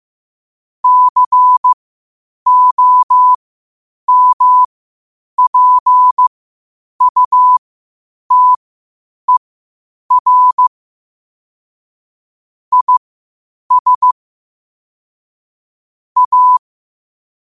Morse code, heard as a keyed tone, about 15 wpm, spaced as for 8 wpm, 1000 Hz. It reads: COMPUTER IS A
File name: qotdfiles / 8wpm